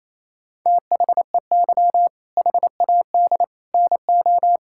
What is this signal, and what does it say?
Morse code, heard as a keyed tone, about 27 wpm, 700 Hz. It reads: THEY HAD NO
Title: Sentences from Top 1000 Words - Review Part 6 - 28wpm